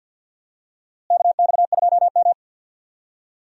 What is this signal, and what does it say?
Morse code, heard as a keyed tone, about 50 wpm, 700 Hz. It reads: KX2M
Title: USA Call Signs - 4 Characters - Repeated 2x - 50wpm